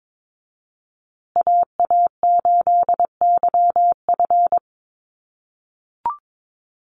Morse code, heard as a keyed tone, about 22 wpm, 700 Hz. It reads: AA8YF
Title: USA Call Signs Repeated 3x 22wpm